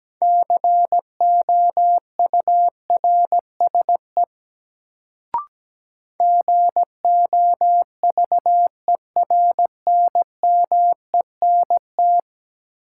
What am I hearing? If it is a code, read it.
Morse code, heard as a keyed tone, about 17 wpm, 700 Hz. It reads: COURSE GOVERNMENT
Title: Sets of 2 Words - New Word Drill - 17wpm